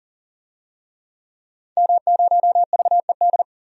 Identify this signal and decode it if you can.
Morse code, heard as a keyed tone, about 40 wpm, 700 Hz. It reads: M0VED
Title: DXCC-100 Call Signs - 5 Characters - 40wpm